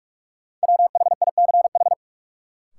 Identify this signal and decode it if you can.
Morse code, heard as a keyed tone, about 45 wpm, 700 Hz. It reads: WHICH
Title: Sets of 2 Words - New Word Drill - 45wpm